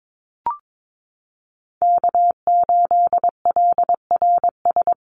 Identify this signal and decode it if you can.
Morse code, heard as a keyed tone, about 22 wpm, 700 Hz. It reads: K8LRS
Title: USA Call Signs Repeated 2x 22wpm